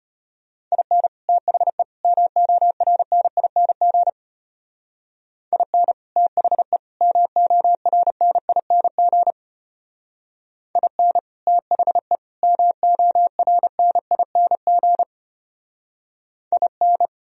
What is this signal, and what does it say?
Morse code, heard as a keyed tone, about 31 wpm, 700 Hz. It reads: IN THE MORNING IN THE MORNING IN THE MORNING IN
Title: Speed-Racing - Sets of 3 Words 25wpm